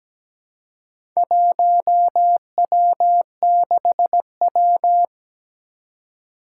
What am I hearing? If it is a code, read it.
Morse code, heard as a keyed tone, about 17 wpm, 700 Hz. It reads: 1W6W
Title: Four Challenging Letter-Number Combinations 17wpm